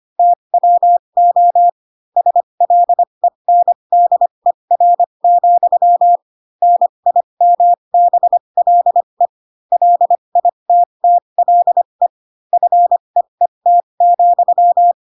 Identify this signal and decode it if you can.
Morse code, heard as a keyed tone, about 25 wpm, spaced as for 22 wpm, 700 Hz. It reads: TWO SLENDER, NIMBLE LITTLE FEET,